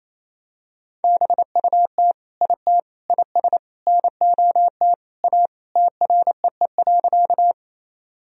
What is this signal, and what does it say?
Morse code, heard as a keyed tone, about 28 wpm, 700 Hz. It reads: BUT IT IS NOT A TREE.
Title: Sentences from Top 1000 Words - Review Part 1 - 28wpm